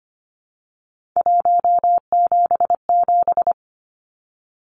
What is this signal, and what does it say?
Morse code, heard as a keyed tone, about 25 wpm, 700 Hz. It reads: 177
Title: Numbers - 3-Digit 25wpm